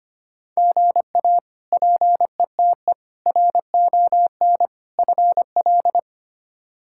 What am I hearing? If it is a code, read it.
Morse code, heard as a keyed tone, about 25 wpm, 700 Hz. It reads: GA PETE RON FL
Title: K1USN SST QSO Practice - 25wpm